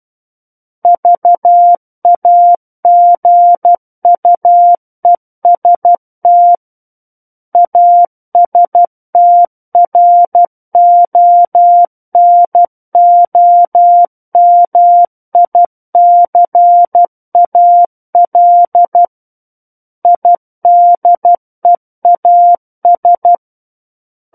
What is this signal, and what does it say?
Morse code, heard as a keyed tone, about 12 wpm, 700 Hz. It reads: VAGUEST ASTRONOMICAL IDEAS
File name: Book One / 03-Chapter 3 - 12 WPM